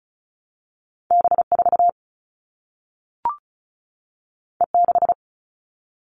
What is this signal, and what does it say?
Morse code, heard as a keyed tone, about 36 wpm, 700 Hz. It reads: B4 E6
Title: All Cut Numbers - 2-Digit 35wpm